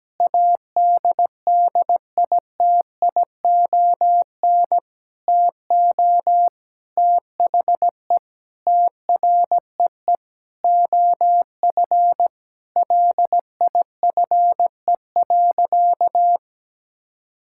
Morse code, heard as a keyed tone, about 17 wpm, 700 Hz. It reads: ADDITION TO THE TREE OF LIFE.
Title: Sentences from Top 1000 Words - Review Part 6 - 17wpm